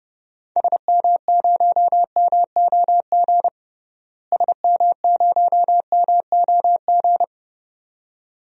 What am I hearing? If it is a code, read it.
Morse code, heard as a keyed tone, about 32 wpm, 700 Hz. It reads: SM0MOG SM0MOG
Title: DXCC-100 Call Signs - 6 Characters - Repeated 2x - 30wpm